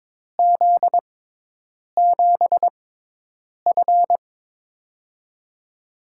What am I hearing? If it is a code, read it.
Morse code, heard as a keyed tone, about 22 wpm, spaced as for 8 wpm, 700 Hz. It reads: Z7F
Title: USA Call Signs - 4 Characters - 6x Character Spacing - 22wpm